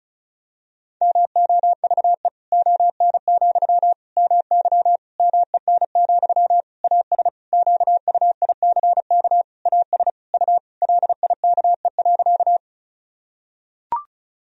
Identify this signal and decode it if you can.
Morse code, heard as a keyed tone, about 36 wpm, 700 Hz. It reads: MOVE ON, MY MEN, AS QUICK AS U LIKE.
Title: Sentences from Top 500 Words and Abbreviations 35wpm